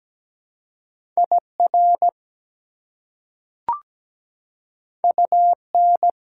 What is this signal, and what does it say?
Morse code, heard as a keyed tone, about 17 wpm, 700 Hz. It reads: IR UN
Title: Prefixes and Suffixes 17wpm